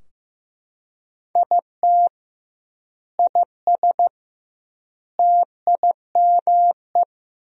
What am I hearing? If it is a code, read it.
Morse code, heard as a keyed tone, about 15 wpm, 700 Hz. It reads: IT IS TIME
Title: 2x Word-Spacing - Sentences from Top 500 Words - 15wpm